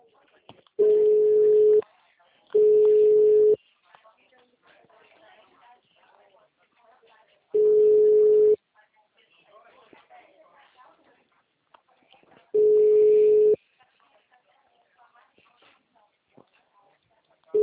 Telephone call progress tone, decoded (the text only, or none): ringback tone